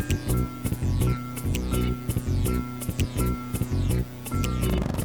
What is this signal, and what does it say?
There is no code.